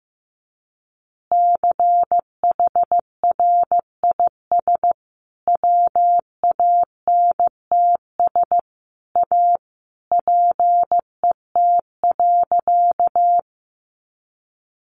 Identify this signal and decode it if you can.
Morse code, heard as a keyed tone, about 15 wpm, 700 Hz. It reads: CHRIS WANTS A PET.
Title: Additional Easy Phrases 15wpm